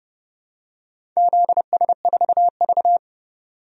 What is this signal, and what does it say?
Morse code, heard as a keyed tone, about 30 wpm, 700 Hz. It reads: ZS4V
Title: DXCC-100 Call Signs - 4 Characters - 30wpm